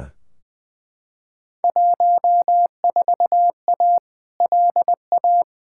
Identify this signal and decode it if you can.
Morse code, heard as a keyed tone, about 20 wpm, 700 Hz. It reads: 14A LA
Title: Field Day Exchanges Encore - 20wpm